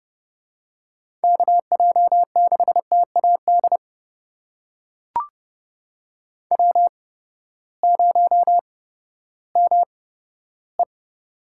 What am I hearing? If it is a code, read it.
Morse code, heard as a keyed tone, about 30 wpm, spaced as for 15 wpm, 700 Hz. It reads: KJ6TAD W0ME
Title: USA Call Signs - 8x Character Spacing - 30wpm